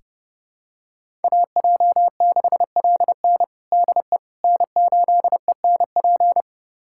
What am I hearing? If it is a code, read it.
Morse code, heard as a keyed tone, about 30 wpm, 700 Hz. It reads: AJ6LN DE N8ENP
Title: USA Callsign DE Callsign - 5 Characters - 30wpm